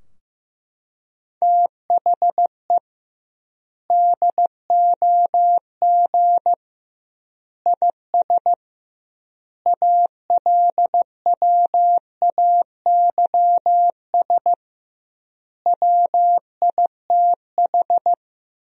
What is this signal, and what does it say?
Morse code, heard as a keyed tone, about 15 wpm, 700 Hz. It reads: THE DOG IS ALWAYS WITH
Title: 2x Word-Spacing - Sentences from Top 400 Words - 15wpm